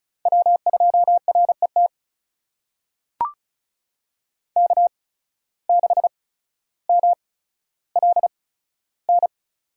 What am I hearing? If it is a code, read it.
Morse code, heard as a keyed tone, about 36 wpm, spaced as for 15 wpm, 700 Hz. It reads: W2RET K6MLN
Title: USA Call Signs - 8x Character Spacing - 35wpm